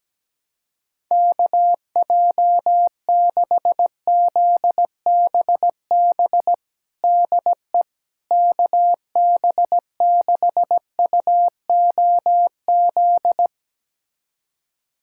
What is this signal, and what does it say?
Morse code, heard as a keyed tone, about 17 wpm, 700 Hz. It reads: KJ6ZBB DE KB6UOZ
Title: USA Callsign DE Callsign 17wpm